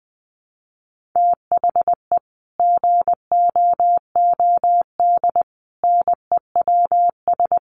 Morse code, heard as a keyed tone, about 20 wpm, 700 Hz. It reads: THE GOOD NEWS
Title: Sets of 5 Words 20wpm